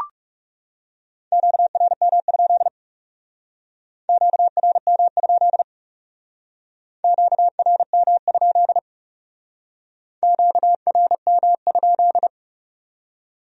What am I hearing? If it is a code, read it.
Morse code, heard as a keyed tone, about 36 wpm, 700 Hz. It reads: QRM? QRM? QRM? QRM?
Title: Speed-Racing Q-Codes - 30wpm